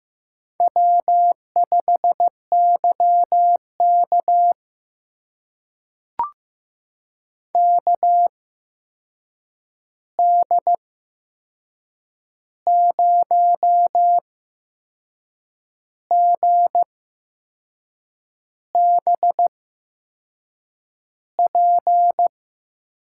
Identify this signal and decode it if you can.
Morse code, heard as a keyed tone, about 15 wpm, spaced as for 6 wpm, 700 Hz. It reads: W5YK KD0GBP
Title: USA Call Signs - 8x Character Spacing - 15wpm